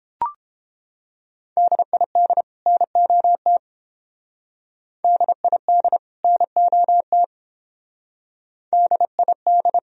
Morse code, heard as a keyed tone, about 30 wpm, 700 Hz. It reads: DID NOT DID NOT DID